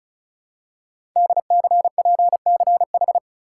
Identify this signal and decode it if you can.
Morse code, heard as a keyed tone, about 36 wpm, 700 Hz. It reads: DCPCH
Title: Code Groups - 5 Letters - 35wpm